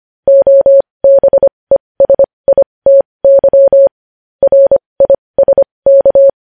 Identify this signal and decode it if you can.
Morse code, heard as a keyed tone, about 25 wpm, spaced as for 20 wpm, 550 Hz. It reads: OBESITY RISK
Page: news-headlines-in-morse-code-at-20-wpm